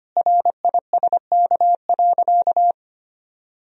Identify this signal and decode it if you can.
Morse code, heard as a keyed tone, about 25 wpm, 700 Hz. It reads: RISK.
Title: Sentences from Top 1000 Words - Review Part 4 - 25wpm